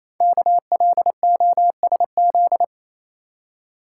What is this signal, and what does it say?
Morse code, heard as a keyed tone, about 27 wpm, 700 Hz. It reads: KLOSZ
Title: Code Groups - 5 Letters - 28wpm